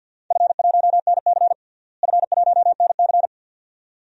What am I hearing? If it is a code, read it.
Morse code, heard as a keyed tone, about 51 wpm, 700 Hz. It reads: F1NC F1NC
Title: DXCC-100 Call Signs - 4 Characters - Repeated 2x - 50wpm